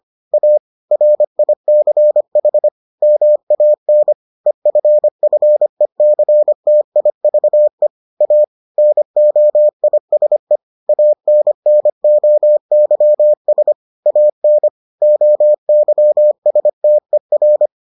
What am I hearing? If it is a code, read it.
Morse code, heard as a keyed tone, about 25 wpm, 600 Hz. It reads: A RICH MAN EFFECTIVE A NOISE ANNOYS AN OYSTER